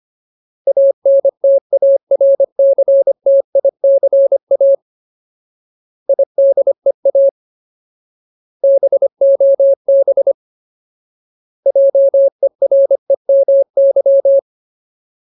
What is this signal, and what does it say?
Morse code, heard as a keyed tone, about 25 wpm, 550 Hz. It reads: ANTARCTICA IDEA BOB JEREMY
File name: Ultimate Word List / 25WPM Ultimate